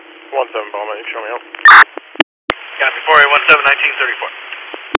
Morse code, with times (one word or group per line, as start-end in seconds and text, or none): none